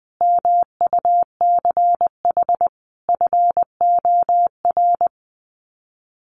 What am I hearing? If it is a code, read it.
Morse code, heard as a keyed tone, about 19 wpm, 700 Hz. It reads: MUCH FOR
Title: Sets of 5 Words 20wpm